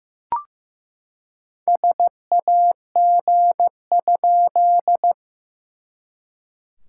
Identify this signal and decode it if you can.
Morse code, heard as a keyed tone, about 14 wpm, 700 Hz. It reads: SAG?
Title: SOTA QSO Practice 15wpm